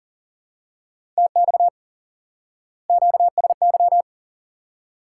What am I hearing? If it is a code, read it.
Morse code, heard as a keyed tone, about 40 wpm, 700 Hz. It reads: TX QSY
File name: IQE Course 3 - Lesson 89 - YAGI - Two Elements - 40wpm